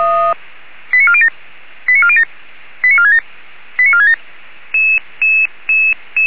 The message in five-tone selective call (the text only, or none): 89407 89407 89467 89467